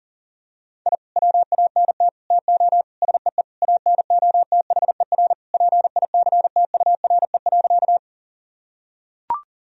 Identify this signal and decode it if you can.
Morse code, heard as a keyed tone, about 40 wpm, 700 Hz. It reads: I WANT TO SEE ANOTHER PICTURE.
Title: Sentences from Top 1000 Words - Review Part 1 - 40wpm